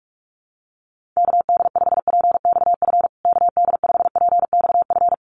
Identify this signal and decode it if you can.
Morse code, heard as a keyed tone, about 44 wpm, 700 Hz. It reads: KD5PXF KD5PXF
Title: USA Call Signs Repeated 3x 45wpm